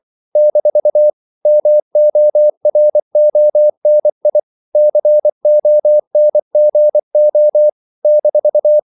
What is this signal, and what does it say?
Morse code, heard as a keyed tone, about 24 wpm, 600 Hz. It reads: - MORONI CONGO -
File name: CW / Practice / NationalCapitals24